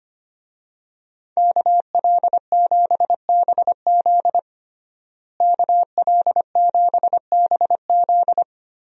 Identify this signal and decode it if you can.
Morse code, heard as a keyed tone, about 25 wpm, 700 Hz. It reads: KL7BZ KL7BZ
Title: DXCC-100 Call Signs - 5 Characters - Repeated 2x - 25wpm